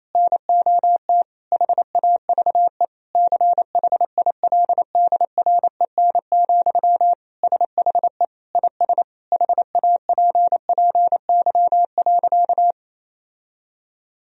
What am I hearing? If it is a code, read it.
Morse code, heard as a keyed tone, about 27 wpm, 700 Hz. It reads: NOT HAVE CHILDREN, SHE IS HAPPY.
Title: Sentences from Top 1000 Words - Review Part 3 - 28wpm